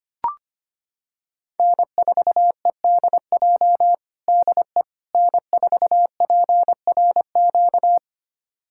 Morse code, heard as a keyed tone, about 25 wpm, 700 Hz. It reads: N4EDJ DE N4PRQ
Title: USA Callsign DE Callsign - 5 Characters - 25wpm